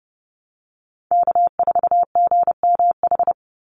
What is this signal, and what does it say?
Morse code, heard as a keyed tone, about 30 wpm, 700 Hz. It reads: K4GMH
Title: Call Signs Repeated 2x - 30wpm